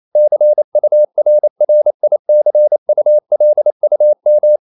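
Morse code, heard as a keyed tone, about 28 wpm, 600 Hz. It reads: CURRICULUM
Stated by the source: Morse code practice at 28 wpm, text of Ken Follett's novel Never in English, from CW med Gnister.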